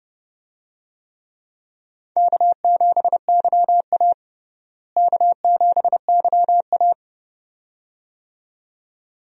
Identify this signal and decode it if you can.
Morse code, heard as a keyed tone, about 30 wpm, 700 Hz. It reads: K7YA K7YA
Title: USA Call Signs - 4 Characters - Repeated 2x - 30wpm